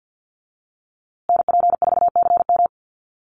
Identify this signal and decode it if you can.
Morse code, heard as a keyed tone, about 50 wpm, 700 Hz. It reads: NP4CM
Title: USA Call Signs Repeated 3x 50wpm